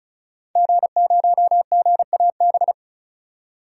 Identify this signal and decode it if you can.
Morse code, heard as a keyed tone, about 36 wpm, 700 Hz. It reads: G0GAB
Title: DXCC-100 Call Signs - 5 Characters - 35wpm